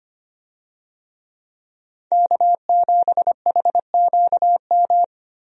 Morse code, heard as a keyed tone, about 25 wpm, 700 Hz. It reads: K7HQM
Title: USA Call Signs - 5 Characters - Repeated 2x - 25wpm